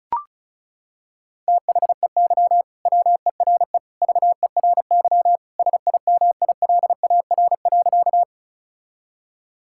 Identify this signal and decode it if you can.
Morse code, heard as a keyed tone, about 34 wpm, 700 Hz. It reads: THEY WERE VERY SIMILAR.
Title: Sentences from Top 1000 Words - Review Part 5 - 35wpm